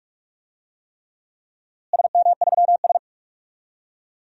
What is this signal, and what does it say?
Morse code, heard as a keyed tone, about 42 wpm, 700 Hz. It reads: SM3S
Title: DXCC-100 Call Signs - 4 Characters - Repeated 3x - 45wpm